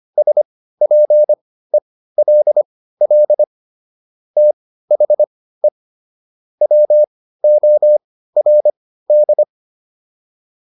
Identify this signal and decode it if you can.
Morse code, heard as a keyed tone, about 25 wpm, spaced as for 15 wpm, 600 Hz. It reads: SPELL THE WORD